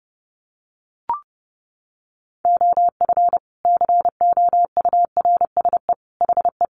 Morse code, heard as a keyed tone, about 30 wpm, 700 Hz. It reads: OF COURSE HE
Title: Sentences from Top 1200 Words 30wpm